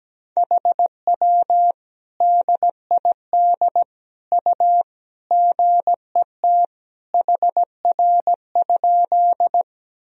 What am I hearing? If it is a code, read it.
Morse code, heard as a keyed tone, about 17 wpm, 700 Hz. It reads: HW DID U GET HR?
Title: Sentences from Top 200 Words and Abbreviations 17wpm